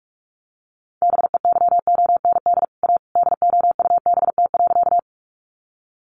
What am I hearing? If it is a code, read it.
Morse code, heard as a keyed tone, about 44 wpm, 700 Hz. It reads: BEYOND A DOUBT.
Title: Sentences from Top 1500 Words 45wpm